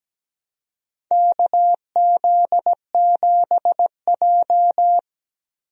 Morse code, heard as a keyed tone, about 17 wpm, 700 Hz. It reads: KZ7J
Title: USA Call Signs - 4 Characters - Repeated 3x - 17wpm